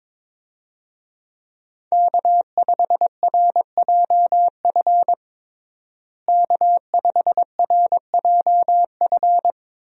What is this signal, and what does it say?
Morse code, heard as a keyed tone, about 22 wpm, 700 Hz. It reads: K5RJF K5RJF
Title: USA Call Signs - 5 Characters - Repeated 2x - 22wpm